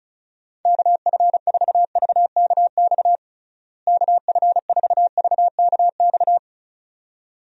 Morse code, heard as a keyed tone, about 35 wpm, 700 Hz. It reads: KF4VKX KF4VKX
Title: USA Call Signs - 6 Characters - Repeated 2x - 35wpm